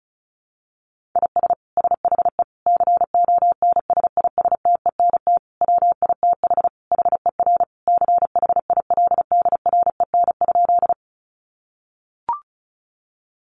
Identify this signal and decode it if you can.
Morse code, heard as a keyed tone, about 34 wpm, 700 Hz. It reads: IS SHE CONSISTENT WITH HER CHILDREN?
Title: Sentences from Top 1500 Words 35wpm